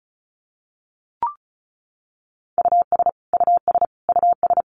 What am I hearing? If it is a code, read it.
Morse code, heard as a keyed tone, about 34 wpm, 700 Hz. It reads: US US US
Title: Top 300 Words Repeated 35wpm